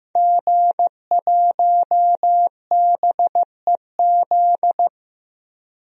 Morse code, heard as a keyed tone, about 15 wpm, 700 Hz. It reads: G1BEZ
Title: DXCC-100 Call Signs - 5 Characters - Repeated 2x - 15wpm